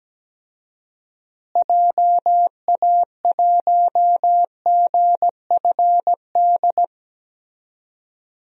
Morse code, heard as a keyed tone, about 17 wpm, 700 Hz. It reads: JA1GFD
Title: DXCC-100 Call Signs - 6 Characters - Repeated 3x - 17wpm